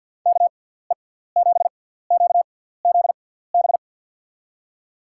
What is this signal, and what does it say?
Morse code, heard as a keyed tone, about 52 wpm, spaced as for 17 wpm, 700 Hz. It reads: KE7QZB
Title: USA Call Signs - 6 Characters - 6x Character Spacing - 50wpm